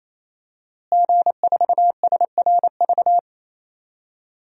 Morse code, heard as a keyed tone, about 28 wpm, 700 Hz. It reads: G4SRV
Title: DXCC-100 Call Signs - 5 Characters - Repeated 3x - 28wpm